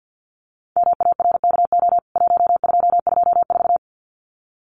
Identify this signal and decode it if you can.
Morse code, heard as a keyed tone, about 50 wpm, 700 Hz. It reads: MARKO 1224
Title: CWT Exchange 50wpm